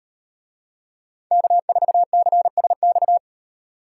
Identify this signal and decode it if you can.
Morse code, heard as a keyed tone, about 37 wpm, 700 Hz. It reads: K4CSX